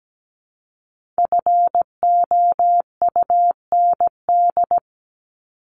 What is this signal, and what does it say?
Morse code, heard as a keyed tone, about 17 wpm, 700 Hz. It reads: FOUND
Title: Top 300 Words Repeated 17wpm